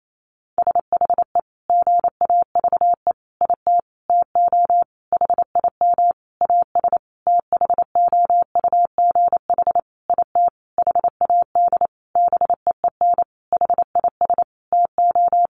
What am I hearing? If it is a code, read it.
Morse code, heard as a keyed tone, about 28 wpm, 700 Hz. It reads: SHE GAVE IT TO HIM AS THOUGH IT HAD BEEN HIS TO
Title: Sentences from Top 1500 Words 28wpm